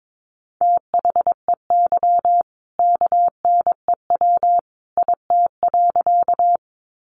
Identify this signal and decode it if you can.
Morse code, heard as a keyed tone, about 22 wpm, 700 Hz. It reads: THEY KNEW IT.
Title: Sentences from Top 400 Words 22wpm